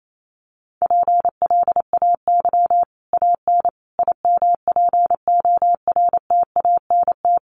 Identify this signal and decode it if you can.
Morse code, heard as a keyed tone, about 28 wpm, 700 Hz. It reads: PLAY AN IMPORTANT
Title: Sets of 5 Words Encore 28wpm